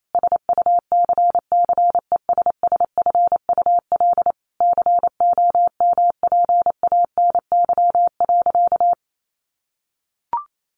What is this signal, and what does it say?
Morse code, heard as a keyed tone, about 28 wpm, 700 Hz. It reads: SUCCESSFUL COMPANY.
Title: Sentences from Top 1100 Words 28wpm